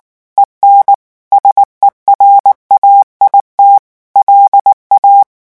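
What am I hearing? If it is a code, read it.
Morse code, heard as a keyed tone, about 19 wpm, 800 Hz. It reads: EN SERAIT LA
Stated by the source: French Morse revision 8 drill of random five-letter groups at 18 words per minute.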